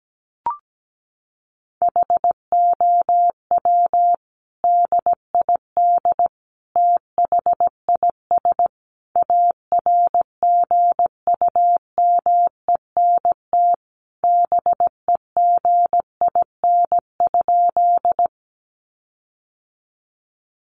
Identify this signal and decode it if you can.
Morse code, heard as a keyed tone, about 17 wpm, 700 Hz. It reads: HOW DID THIS ARGUMENT BEGIN?
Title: Sentences from Top 1400 Words 17wpm